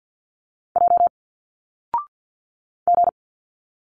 Morse code, heard as a keyed tone, about 50 wpm, 700 Hz. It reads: J G
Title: Single Letters 50wpm